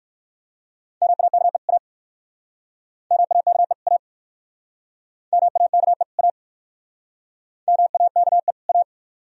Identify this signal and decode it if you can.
Morse code, heard as a keyed tone, about 59 wpm, 690 Hz. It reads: MAKE A MAKE A MAKE A MAKE A